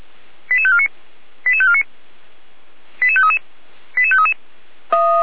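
Five-tone selective call (ZVEI), heard as quick runs of five tones, tones of about 70 ms. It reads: 88549 88549 88533 88533
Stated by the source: ZVEI five-tone selective call 88335 received on an SDR.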